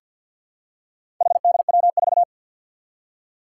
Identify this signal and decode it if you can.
Morse code, heard as a keyed tone, about 52 wpm, 700 Hz. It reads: HDW4
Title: Four Challenging Letter-Number Combinations 50wpm